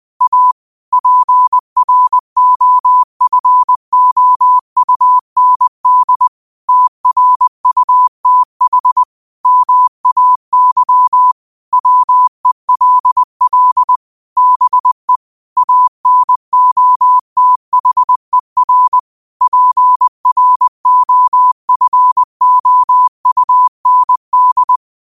Morse code, heard as a keyed tone, about 20 wpm, 1000 Hz. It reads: A PROFOUND TRUTH MAY WELL BE ANOTHER PROFOUND
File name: qotdfiles / 20wpm